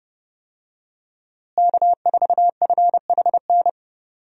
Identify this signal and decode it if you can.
Morse code, heard as a keyed tone, about 30 wpm, 700 Hz. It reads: K4FHN
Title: USA Call Signs - 5 Characters - Repeated 2x - 30wpm